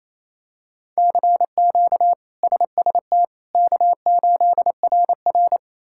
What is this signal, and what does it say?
Morse code, heard as a keyed tone, about 28 wpm, 700 Hz. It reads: CQ SST K8RR
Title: K1USN SST QSO Practice - 28wpm